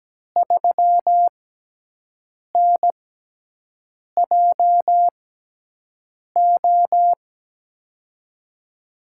Morse code, heard as a keyed tone, about 17 wpm, spaced as for 6 wpm, 700 Hz. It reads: 3NJO
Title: USA Call Signs - 5 Characters - 6x Character Spacing - 17wpm